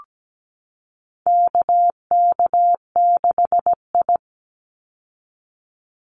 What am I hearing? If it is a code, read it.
Morse code, heard as a keyed tone, about 17 wpm, 700 Hz. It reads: KK6I
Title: Call Signs Repeated 2x - 17wpm